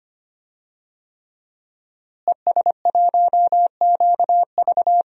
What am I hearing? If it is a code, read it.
Morse code, heard as a keyed tone, about 25 wpm, 700 Hz. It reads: ES1QV
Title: DXCC-100 Call Signs - 5 Characters - Repeated 3x - 25wpm